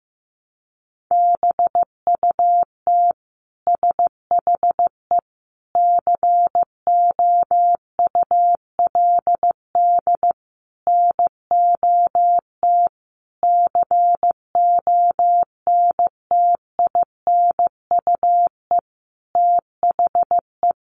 Morse code, heard as a keyed tone, about 15 wpm, 700 Hz. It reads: BUT SHE COULD NOT CONTINUE THE
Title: Sentences from Top 1200 Words 15wpm